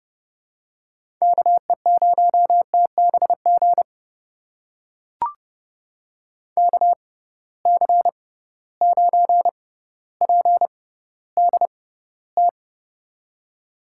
Morse code, heard as a keyed tone, about 32 wpm, spaced as for 16 wpm, 700 Hz. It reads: KE0TBG KC9PDT
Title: USA Call Signs - 6 Characters - 6x Character Spacing - 30wpm